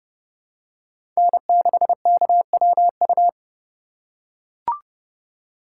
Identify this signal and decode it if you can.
Morse code, heard as a keyed tone, about 31 wpm, 700 Hz. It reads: N6KWU
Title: USA Call Signs - 5 Characters - 2x Character Spacing - 30wpm